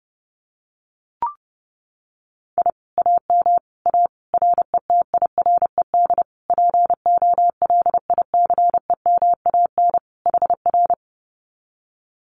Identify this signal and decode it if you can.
Morse code, heard as a keyed tone, about 30 wpm, 700 Hz. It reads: I AM A RETIRED POLICEMAN HR
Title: Common QSO Phrases Encore 30wpm